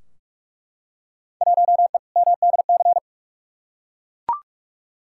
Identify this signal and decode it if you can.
Morse code, heard as a keyed tone, about 42 wpm, 700 Hz. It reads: 1E MDC
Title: Field Day Exchanges Encore - 45wpm